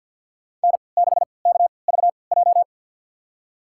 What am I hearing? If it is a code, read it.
Morse code, heard as a keyed tone, about 51 wpm, spaced as for 28 wpm, 700 Hz. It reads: N6KVJ
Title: USA Call Signs - 3x Character Spacing - 50wpm